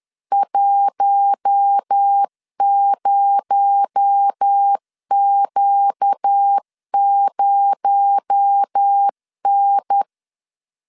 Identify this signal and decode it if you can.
Morse code, heard as a keyed tone, about 10 wpm, 800 Hz. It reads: 10Q0N